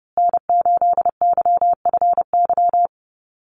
Morse code, heard as a keyed tone, about 29 wpm, 700 Hz. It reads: N8YFY
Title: USA Call Signs Repeated 2x 30wpm